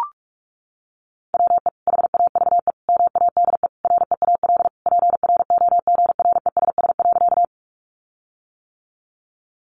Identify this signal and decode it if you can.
Morse code, heard as a keyed tone, about 44 wpm, 700 Hz. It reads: WE HAVE MADE REAL PROGRESS.
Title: Sentences from Top 1400 Words 45wpm